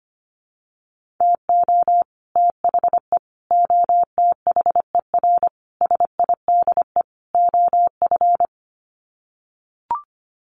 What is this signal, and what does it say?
Morse code, heard as a keyed tone, about 25 wpm, 700 Hz. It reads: TO THE OTHER SIDE OF
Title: Sets of 5 Words Encore 25wpm